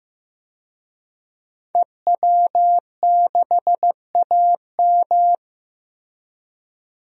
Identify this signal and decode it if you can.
Morse code, heard as a keyed tone, about 15 wpm, 700 Hz. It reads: EW6AM
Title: DXCC-100 Call Signs - 5 Characters - Repeated 3x - 15wpm